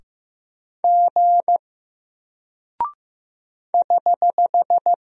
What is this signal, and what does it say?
Morse code, heard as a keyed tone, about 15 wpm, 700 Hz. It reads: G <HH>
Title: Single Character (Letter:Number:Punctuation:Prosign) Alt 15wpm